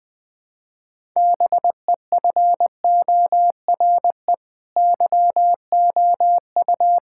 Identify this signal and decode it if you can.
Morse code, heard as a keyed tone, about 20 wpm, 700 Hz. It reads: BEFORE YOU
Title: Sets of 9 Words - 20wpm